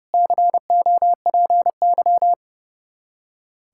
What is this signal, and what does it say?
Morse code, heard as a keyed tone, about 30 wpm, 700 Hz. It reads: COPY
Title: Practice QSO - Exchange 3 - More Info - 30wpm